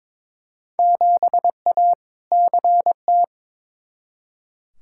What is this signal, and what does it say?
Morse code, heard as a keyed tone, about 22 wpm, 700 Hz. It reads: 7A CT
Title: Field Day Exchanges - 22wpm